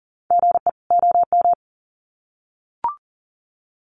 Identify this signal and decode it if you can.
Morse code, heard as a keyed tone, about 40 wpm, 700 Hz. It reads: GE OM
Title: Common QSO Phrases 40wpm